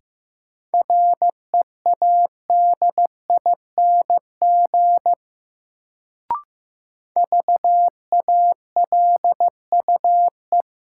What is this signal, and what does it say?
Morse code, heard as a keyed tone, about 16 wpm, 700 Hz. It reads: READING VALUE
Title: Sentences from Top 600 Words - New Word Drill - 15wpm